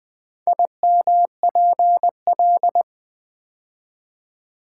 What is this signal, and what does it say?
Morse code, heard as a keyed tone, about 20 wpm, 700 Hz. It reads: IMPL
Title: Build A Word - 4 Letters At A Time - 11-Letter Words - 20wpm